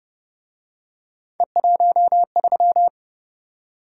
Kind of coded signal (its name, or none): Morse code